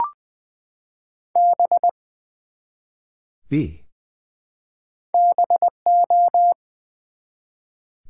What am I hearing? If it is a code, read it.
Morse code, heard as a keyed tone, about 20 wpm, 700 Hz. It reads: B BO
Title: Build A Word - 1 Letter At A Time - 5-Letter Words - 20wpm